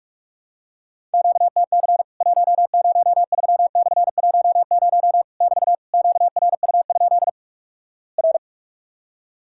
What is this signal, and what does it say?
Morse code, heard as a keyed tone, about 46 wpm, 700 Hz. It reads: QTC 103/10 <BT> QRU? R
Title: Worked All Europe DX Contest - QTC Exchange - 45wpm